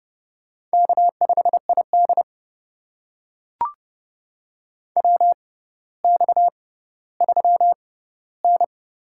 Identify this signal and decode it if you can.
Morse code, heard as a keyed tone, about 30 wpm, spaced as for 15 wpm, 700 Hz. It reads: K5ID WX3N
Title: USA Call Signs - 4 Characters - 6x Character Spacing - 30wpm